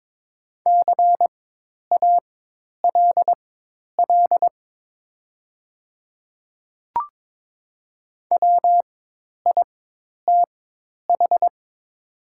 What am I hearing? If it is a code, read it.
Morse code, heard as a keyed tone, about 22 wpm, spaced as for 10 wpm, 700 Hz. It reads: CALL WITH
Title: 4x Character Spacing - Top 100 Words - 22wpm